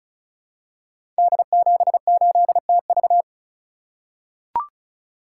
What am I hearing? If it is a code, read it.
Morse code, heard as a keyed tone, about 36 wpm, 700 Hz. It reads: D78TV
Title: Code Groups - 5 Letter:Numbers - 35wpm